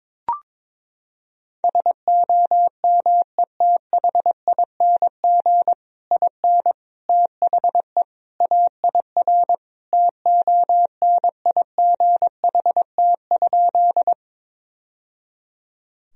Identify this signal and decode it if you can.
Morse code, heard as a keyed tone, about 21 wpm, 700 Hz. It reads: SOMETHING IN THE AIR TONIGHT?
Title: Sentences from Top 1000 Words - Review Part 1 - 22wpm